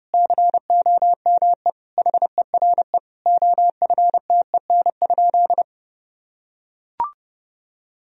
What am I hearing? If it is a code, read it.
Morse code, heard as a keyed tone, about 30 wpm, 700 Hz. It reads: COME HERE OFTEN?
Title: Sentences from Top 1000 Words - Review Part 4 - 30wpm